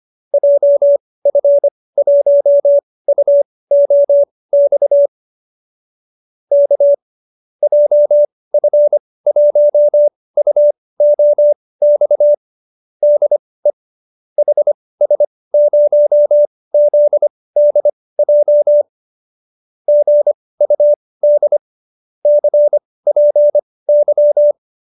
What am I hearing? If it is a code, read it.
Morse code, heard as a keyed tone, about 25 wpm, spaced as for 18 wpm, 590 Hz. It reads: JF1UOX K JF1UOX DE HS0ZDJ GUD CPY